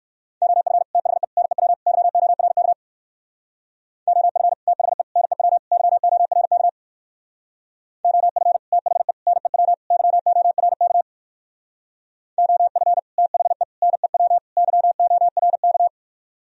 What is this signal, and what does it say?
Morse code, heard as a keyed tone, about 58 wpm, 700 Hz. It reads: OF THE NEW YORK OF THE NEW YORK OF THE NEW YORK OF THE NEW YORK